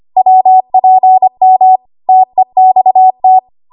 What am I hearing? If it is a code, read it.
Morse code, heard as a keyed tone, about 25 wpm, 750 Hz. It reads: WPM TEXT
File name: Morse / 25WPM